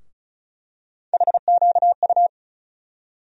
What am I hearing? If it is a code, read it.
Morse code, heard as a keyed tone, about 36 wpm, 700 Hz. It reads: HQU
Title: USA Call Sign Postfixes - 3 Letters 35wpm